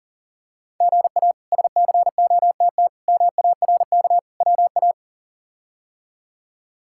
Morse code, heard as a keyed tone, about 40 wpm, 700 Hz. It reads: GA SCOTT MARK WA
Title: K1USN SST QSO Practice - 40wpm